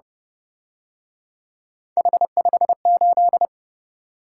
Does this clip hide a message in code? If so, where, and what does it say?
Yes: Morse code from 2.0 to 3.5 s, heard as a keyed tone, about 29 wpm, 700 Hz. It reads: H58